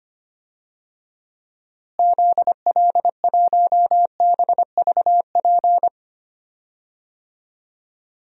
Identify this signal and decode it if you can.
Morse code, heard as a keyed tone, about 25 wpm, 700 Hz. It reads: ZL1BVP
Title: DXCC-100 Call Signs - 6 Characters - Repeated 2x - 25wpm